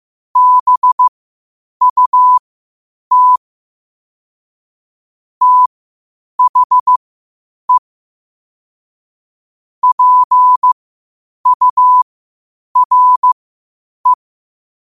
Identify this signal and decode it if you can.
Morse code, heard as a keyed tone, about 15 wpm, spaced as for 8 wpm, 1000 Hz. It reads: BUT THE PURE
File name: qotdfiles / 8wpm